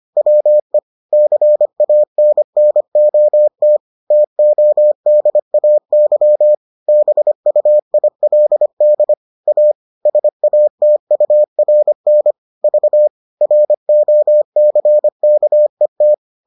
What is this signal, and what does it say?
Morse code, heard as a keyed tone, about 25 wpm, 600 Hz. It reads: WE CANNOT TODAY BUILD A SATURN V ROCKET